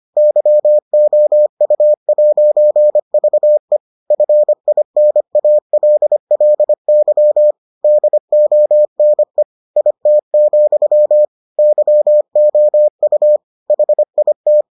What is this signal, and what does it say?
Morse code, heard as a keyed tone, about 25 wpm, 600 Hz. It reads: YOU'VE FINALLY DONE IT, YOU HIT